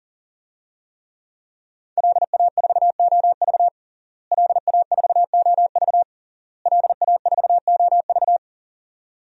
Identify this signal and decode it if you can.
Morse code, heard as a keyed tone, about 40 wpm, 700 Hz. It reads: LA4OV LA4OV LA4OV
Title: DXCC-100 Call Signs - 5 Characters - Repeated 3x - 40wpm